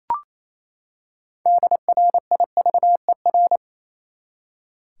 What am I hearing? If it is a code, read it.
Morse code, heard as a keyed tone, about 27 wpm, 700 Hz. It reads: DRIVER
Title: Six Letter Words - 28wpm